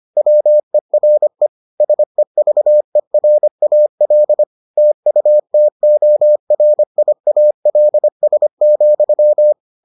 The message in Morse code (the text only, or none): WERE SEVERAL TUTORIALS,